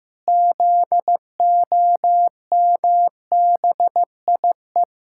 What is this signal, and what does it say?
Morse code, heard as a keyed tone, about 15 wpm, 700 Hz. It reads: ZOMBIE
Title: Pangram Challenge v2 - 15wpm